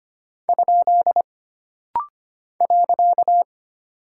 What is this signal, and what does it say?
Morse code, heard as a keyed tone, about 25 wpm, 700 Hz. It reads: ? .
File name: Single Punctuation-Prosign - Rapid-Fire - 25wpm